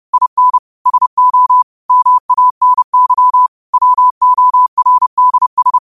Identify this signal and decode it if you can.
Morse code, heard as a keyed tone, about 30 wpm, 1000 Hz. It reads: IN SO MANY WORDS
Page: New quotes every day in morse code at 30 Words per minute.